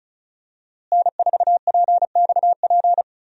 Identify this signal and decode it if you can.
Morse code, heard as a keyed tone, about 34 wpm, 700 Hz. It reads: N4PXP
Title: USA Call Signs - 5 Characters - Repeated 3x - 35wpm